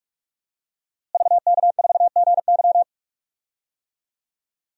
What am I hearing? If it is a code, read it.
Morse code, heard as a keyed tone, about 44 wpm, 700 Hz. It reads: VK4GY
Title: DXCC-100 Call Signs - 5 Characters - Repeated 2x - 45wpm